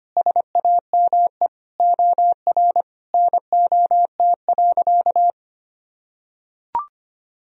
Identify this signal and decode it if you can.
Morse code, heard as a keyed tone, about 25 wpm, 700 Hz. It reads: SAME OR NOT.
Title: Sentences from Top 1000 Words - Review Part 4 - 25wpm